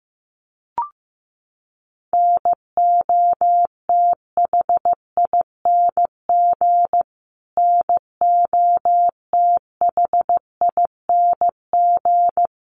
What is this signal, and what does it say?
Morse code, heard as a keyed tone, about 15 wpm, 700 Hz. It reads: NOTHING NOTHING
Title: Top 500 Words Repeated 15wpm